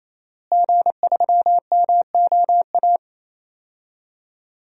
Morse code, heard as a keyed tone, about 28 wpm, 700 Hz. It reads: G3MOA
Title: DXCC-100 Call Signs - 5 Characters - Repeated 3x - 28wpm